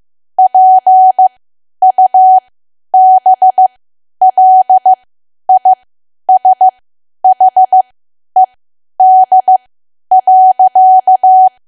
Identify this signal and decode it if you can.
Morse code, heard as a keyed tone, about 15 wpm, spaced as for 10 wpm, 750 Hz. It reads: PUBLISHED.
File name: cwa / W1AW-Level2 / 10WPM